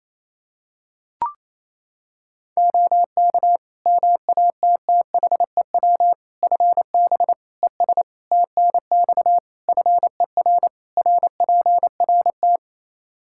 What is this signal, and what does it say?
Morse code, heard as a keyed tone, about 28 wpm, 700 Hz. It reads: OK MATTHEW FB ES TNX FER RPRT
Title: Practice QSO - Exchange 2 - Info - 28wpm